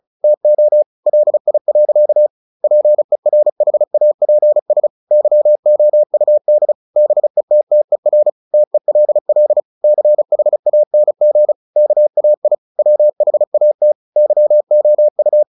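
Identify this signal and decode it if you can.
Morse code, heard as a keyed tone, about 34 wpm, 600 Hz. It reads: TO LI. PERHAPS YOUD BETTER TELL CHANG KAI WHAT YOU
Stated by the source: Morse practice at 35 wpm from CW med Gnister, sending Ken Follett's novel Never in English.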